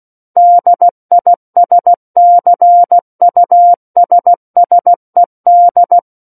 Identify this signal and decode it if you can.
Morse code, heard as a keyed tone, about 16 wpm, 700 Hz. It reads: DISCUSSED
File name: Book One / 07-Chapter 7 - 16 WPM